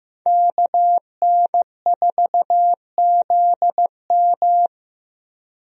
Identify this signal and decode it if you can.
Morse code, heard as a keyed tone, about 15 wpm, 700 Hz. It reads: KN4ZM
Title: USA Call Signs - 5 Characters - Repeated 2x - 15wpm